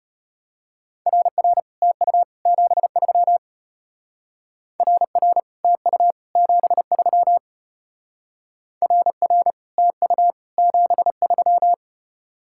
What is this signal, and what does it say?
Morse code, heard as a keyed tone, about 32 wpm, 700 Hz. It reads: RR TU 73 RR TU 73 RR TU 73